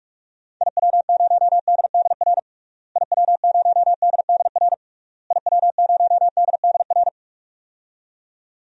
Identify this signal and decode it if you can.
Morse code, heard as a keyed tone, about 46 wpm, 700 Hz. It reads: IW0DDR IW0DDR IW0DDR
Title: DXCC-100 Call Signs - 6 Characters - Repeated 3x - 45wpm